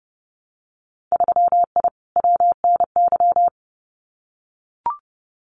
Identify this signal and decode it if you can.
Morse code, heard as a keyed tone, about 30 wpm, 700 Hz. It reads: 3I WNY
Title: Winter Field Day Exchanges Encore 30wpm